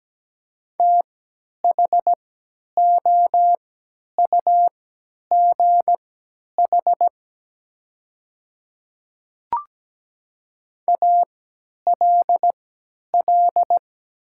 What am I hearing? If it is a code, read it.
Morse code, heard as a keyed tone, about 17 wpm, spaced as for 10 wpm, 700 Hz. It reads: THOUGH ALL
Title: 3x Character Spacing - Top 500 Words - 17wpm